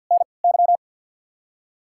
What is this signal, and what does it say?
Morse code, heard as a keyed tone, about 47 wpm, spaced as for 28 wpm, 700 Hz. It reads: NY
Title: 3x Character Spacing - Top 500 Words - 50wpm